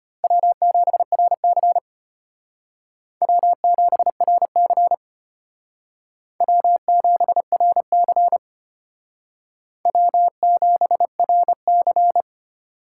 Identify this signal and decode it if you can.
Morse code, heard as a keyed tone, about 32 wpm, 700 Hz. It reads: W7RC W7RC W7RC W7RC